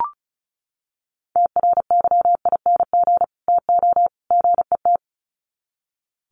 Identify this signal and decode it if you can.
Morse code, heard as a keyed tone, about 34 wpm, 700 Hz. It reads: TRYING TO GET
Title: Sets of 3 Words 35wpm